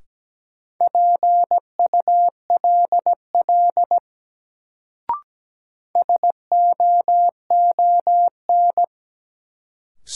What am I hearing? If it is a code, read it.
Morse code, heard as a keyed tone, about 17 wpm, 700 Hz. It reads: PULL SOON
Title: Sentences from Top 400 Words - New Word Drill - 17wpm